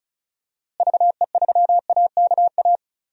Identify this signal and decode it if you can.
Morse code, heard as a keyed tone, about 36 wpm, 700 Hz. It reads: VE3AKA
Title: DXCC-100 Call Signs - 6 Characters - Repeated 3x - 35wpm